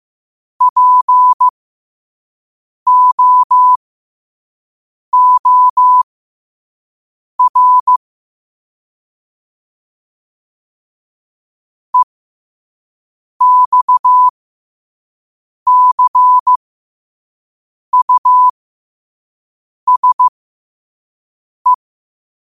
Morse code, heard as a keyed tone, about 15 wpm, spaced as for 5 wpm, 1000 Hz. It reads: POOR EXCUSE
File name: qotdfiles / 5wpm